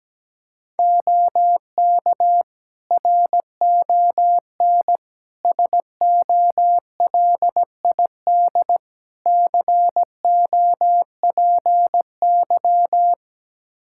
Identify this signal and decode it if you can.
Morse code, heard as a keyed tone, about 17 wpm, 700 Hz. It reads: OK RON SOLID COPY
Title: Practice QSO - Exchange 3 - More Info - 17wpm